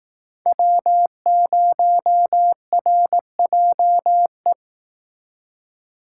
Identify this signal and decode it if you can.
Morse code, heard as a keyed tone, about 18 wpm, 700 Hz. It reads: W0RJE
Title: Speed Racing - USA Call Signs 15wpm